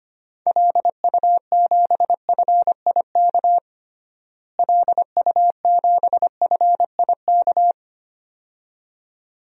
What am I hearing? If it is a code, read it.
Morse code, heard as a keyed tone, about 25 wpm, 700 Hz. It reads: LU7FIK LU7FIK
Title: DXCC-100 Call Signs - 6 Characters - Repeated 2x - 25wpm